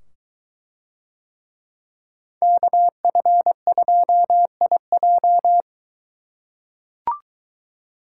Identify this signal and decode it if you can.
Morse code, heard as a keyed tone, about 23 wpm, 700 Hz. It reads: KF2IJ